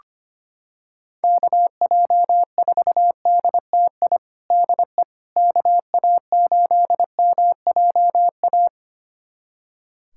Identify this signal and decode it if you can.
Morse code, heard as a keyed tone, about 25 wpm, 700 Hz. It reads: KJ4DTI DE KA8MJA
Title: USA Callsign DE Callsign 25wpm